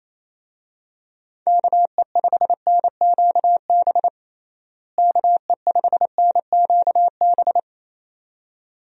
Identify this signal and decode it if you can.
Morse code, heard as a keyed tone, about 28 wpm, 700 Hz. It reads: KE5NQB KE5NQB
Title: USA Call Signs - 6 Characters - Repeated 2x - 28wpm